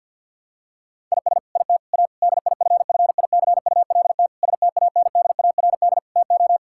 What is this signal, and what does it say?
Morse code, heard as a keyed tone, about 50 wpm, 700 Hz. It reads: IS IT A DIFFICULT STANDARD TO
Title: Sentences from Top 1000 Words - Review Part 6 - 50wpm